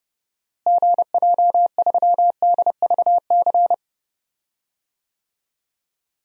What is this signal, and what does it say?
Morse code, heard as a keyed tone, about 31 wpm, 700 Hz. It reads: GJ3DVC
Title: DXCC-100 Call Signs - 6 Characters - Repeated 2x - 30wpm